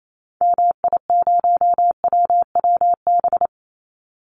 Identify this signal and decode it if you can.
Morse code, heard as a keyed tone, about 28 wpm, 700 Hz. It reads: MI0WWB
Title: Call Signs - 28wpm